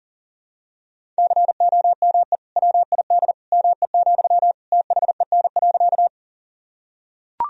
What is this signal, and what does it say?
Morse code, heard as a keyed tone, about 40 wpm, 700 Hz. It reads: COME WID ME, THEN.
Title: Sentences from Top 200 Words and Abbreviations 40wpm